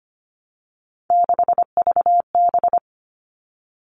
Morse code, heard as a keyed tone, about 25 wpm, 700 Hz. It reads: 6VB
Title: All Cut Numbers - 3-Digit 25wpm